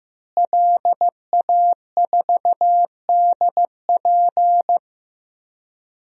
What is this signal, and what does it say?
Morse code, heard as a keyed tone, about 15 wpm, 700 Hz. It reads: LA4DP
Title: DXCC-100 Call Signs - 5 Characters - Repeated 2x - 15wpm